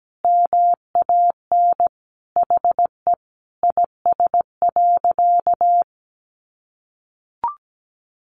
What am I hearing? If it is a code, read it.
Morse code, heard as a keyed tone, about 17 wpm, 700 Hz. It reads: MAN HE IS.
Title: Sentences from Top 1100 Words 17wpm